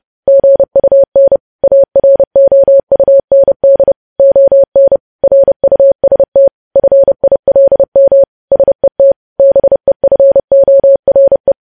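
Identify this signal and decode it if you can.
Morse code, heard as a keyed tone, about 30 wpm, 550 Hz. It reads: GUN AROUND ON RUST FILM SET BEFORE
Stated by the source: Morse code news headlines sent at 30 words per minute.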